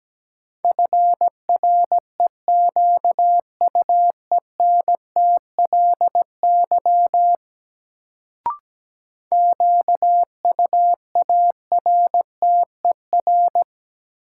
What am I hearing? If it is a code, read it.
Morse code, heard as a keyed tone, about 17 wpm, 700 Hz. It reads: FREQUENTLY QUARTER
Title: Sentences from Top 1300 Words - New Word Drill - 17wpm